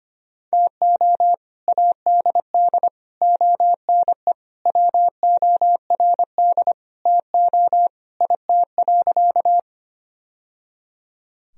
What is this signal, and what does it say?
Morse code, heard as a keyed tone, about 25 wpm, 700 Hz. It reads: TO ADD ONE WORD TO IT.
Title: Sentences from Top 1000 Words - Review Part 1 - 25wpm